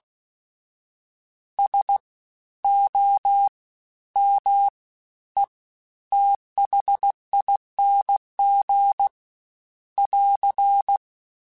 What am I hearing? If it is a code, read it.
Morse code, heard as a keyed tone, about 16 wpm, 800 Hz. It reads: S O M E THING <AR>
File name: Special INGs-mp3s / Special INGs-mp3s / Fast ing @15